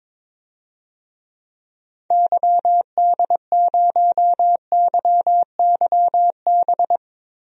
Morse code, heard as a keyed tone, about 22 wpm, 700 Hz. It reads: YD0YYB
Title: DXCC-100 Call Signs - 6 Characters - Repeated 3x - 22wpm